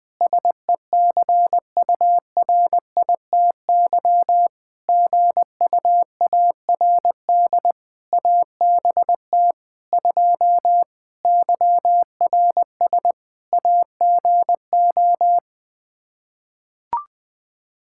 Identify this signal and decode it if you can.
Morse code, heard as a keyed tone, about 21 wpm, 700 Hz. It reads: SECURITY GUARD ABT 2 YRS AGO
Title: Practice QSO - Exchange 3 - More Info - 20wpm